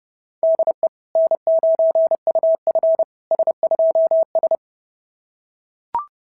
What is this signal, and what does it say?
Morse code, heard as a keyed tone, about 30 wpm, 660 Hz. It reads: DE N9UF S2S
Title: SOTA QSO Practice 30wpm